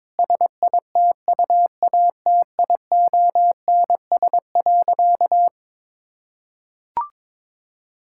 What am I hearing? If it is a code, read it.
Morse code, heard as a keyed tone, about 21 wpm, 700 Hz. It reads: SITUATIONS.
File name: Difficult Sentences with Hints - 22wpm